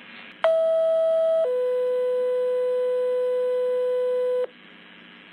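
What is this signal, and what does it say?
Two-tone page, heard as a one-second tone, then a three-second tone, about 650 then 500 Hz, with noise. It is heard from 0.4 to 4.5 s.
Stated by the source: Pager alert ringtone.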